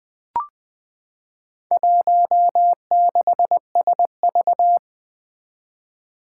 Morse code, heard as a keyed tone, about 20 wpm, 700 Hz. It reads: 16SV
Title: Four Challenging Letter-Number Combinations 20wpm